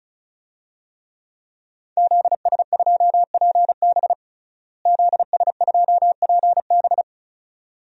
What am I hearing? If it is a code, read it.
Morse code, heard as a keyed tone, about 36 wpm, 700 Hz. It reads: ZS2PB ZS2PB
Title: DXCC-100 Call Signs - 5 Characters - Repeated 2x - 35wpm